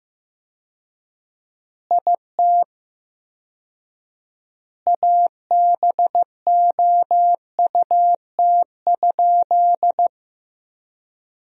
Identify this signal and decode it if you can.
Morse code, heard as a keyed tone, about 16 wpm, 700 Hz. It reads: IT ABOUT?
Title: 4x Word-Spacing - Sentences from Top 100 Words - 15wpm